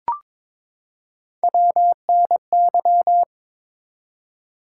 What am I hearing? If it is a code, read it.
Morse code, heard as a keyed tone, about 22 wpm, 700 Hz. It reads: WNY
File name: ARRL Sections v2 - 22wpm